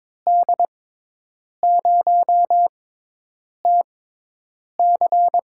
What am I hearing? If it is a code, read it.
Morse code, heard as a keyed tone, about 23 wpm, spaced as for 8 wpm, 700 Hz. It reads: D0TC
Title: USA Call Signs - 6 Characters - 6x Character Spacing - 22wpm